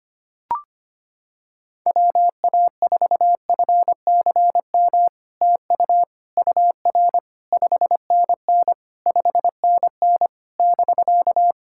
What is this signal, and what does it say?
Morse code, heard as a keyed tone, about 25 wpm, 700 Hz. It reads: WA4FCM TU UR 5NN 5NN <BK>
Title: SOTA QSO Practice 25wpm